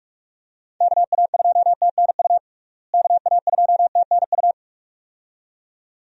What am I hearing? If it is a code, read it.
Morse code, heard as a keyed tone, about 41 wpm, 700 Hz. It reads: KA2TNU KA2TNU
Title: USA Call Signs - 6 Characters - Repeated 2x - 45wpm